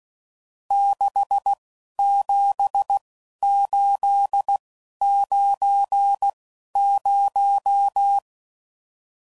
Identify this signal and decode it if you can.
Morse code, heard as a keyed tone, about 16 wpm, spaced as for 11 wpm, 800 Hz. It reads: 67890